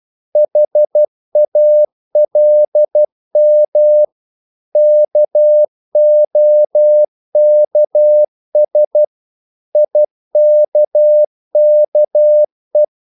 Morse code, heard as a keyed tone, about 12 wpm, 600 Hz. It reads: HALM KOKS IKKE
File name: Korte ord DK 12wpm